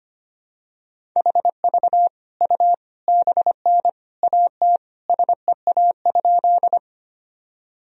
Morse code, heard as a keyed tone, about 25 wpm, 700 Hz. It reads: HV U BN AT SEA?
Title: Sentences from Top 300 Words and Abbreviations 25wpm